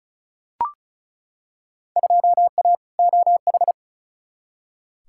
Morse code, heard as a keyed tone, about 36 wpm, 700 Hz. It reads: 2A OH
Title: Field Day Exchanges - 35wpm